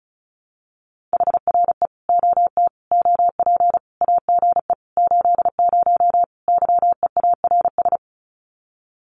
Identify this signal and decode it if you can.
Morse code, heard as a keyed tone, about 34 wpm, 700 Hz. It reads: HRE OT OP AGE 80 YEARS
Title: Common QSO Phrases 35wpm